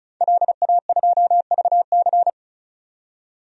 Morse code, heard as a keyed tone, about 36 wpm, 700 Hz. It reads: LA2VC
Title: DXCC-100 Call Signs - 5 Characters - 35wpm